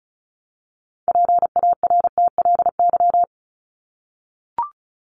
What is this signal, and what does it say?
Morse code, heard as a keyed tone, about 34 wpm, 700 Hz. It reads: PARTLY
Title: LY Suffix - Top 100 Words with Base Words 35wpm